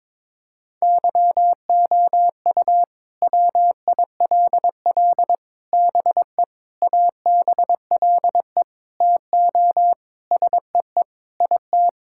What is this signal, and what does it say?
Morse code, heard as a keyed tone, about 22 wpm, 700 Hz. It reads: YOU WILL BE ABLE TO SEE IT
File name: practice / 22wpm / Sentences with 4-Word Phrases - 22wpm